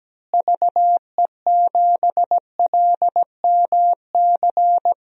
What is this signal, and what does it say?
Morse code, heard as a keyed tone, about 17 wpm, 700 Hz. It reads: VE7LMC
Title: DXCC-100 Call Signs - 6 Characters - Repeated 3x - 17wpm